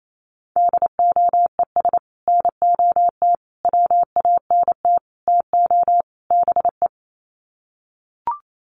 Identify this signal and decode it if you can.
Morse code, heard as a keyed tone, about 28 wpm, 700 Hz. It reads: DOES NOT WANT TO BE
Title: Sets of 5 Words Encore 28wpm